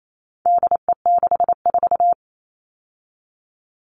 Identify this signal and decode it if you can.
Morse code, heard as a keyed tone, about 28 wpm, 700 Hz. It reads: DE64
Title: All Cut Numbers - 4-Digit 28wpm